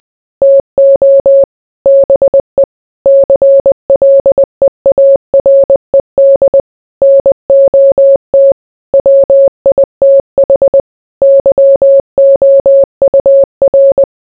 Morse code, heard as a keyed tone, about 20 wpm, 550 Hz. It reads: TO BE CLEARED NOT WITH YOUR